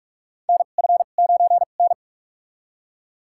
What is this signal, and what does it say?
Morse code, heard as a keyed tone, about 41 wpm, spaced as for 31 wpm, 700 Hz. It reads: NF9N
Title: USA Call Signs - 4 Characters - 2x Character Spacing - 45wpm